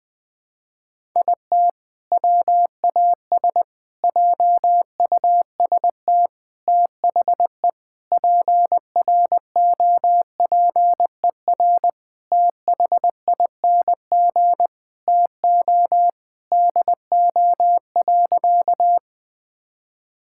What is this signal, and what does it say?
Morse code, heard as a keyed tone, about 20 wpm, 700 Hz. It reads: IT WAS JUST THE PROPER THING TO DO.
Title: Sentences from Top 1000 Words - Review Part 5 - 20wpm